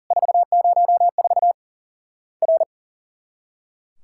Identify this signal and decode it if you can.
Morse code, heard as a keyed tone, about 40 wpm, 700 Hz. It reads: 404 R
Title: Worked All Europe DX Contest - QTC Exchange - 40wpm